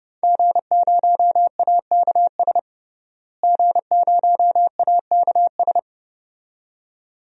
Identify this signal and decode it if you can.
Morse code, heard as a keyed tone, about 32 wpm, 700 Hz. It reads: G0AKS G0AKS
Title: DXCC-100 Call Signs - 5 Characters - Repeated 2x - 30wpm